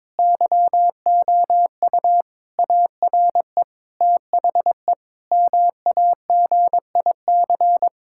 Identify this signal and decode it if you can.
Morse code, heard as a keyed tone, about 22 wpm, 700 Hz. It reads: YOU ARE THE MAGIC
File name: Difficult Sentences with Hints - 22wpm